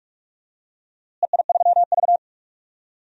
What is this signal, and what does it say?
Morse code, heard as a keyed tone, about 42 wpm, 700 Hz. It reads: EI3V
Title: DXCC-100 Call Signs - 4 Characters - Repeated 2x - 45wpm